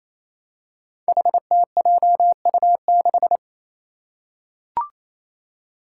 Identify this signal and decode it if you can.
Morse code, heard as a keyed tone, about 28 wpm, 700 Hz. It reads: HTJU6
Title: Five Challenging Letter-Number Combinations 28wpm